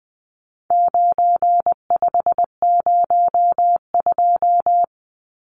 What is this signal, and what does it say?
Morse code, heard as a keyed tone, about 20 wpm, 700 Hz. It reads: 9502
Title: Numbers - 4-Digit 20wpm